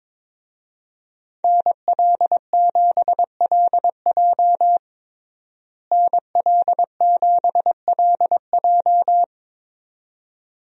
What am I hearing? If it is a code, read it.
Morse code, heard as a keyed tone, about 22 wpm, 700 Hz. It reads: NL7LJ NL7LJ
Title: USA Call Signs - 5 Characters - Repeated 2x - 22wpm